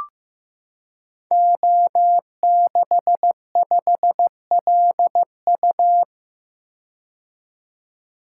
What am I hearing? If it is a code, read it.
Morse code, heard as a keyed tone, about 15 wpm, 700 Hz. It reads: O65LU
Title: Code Groups - 5 Letter:Numbers - 15wpm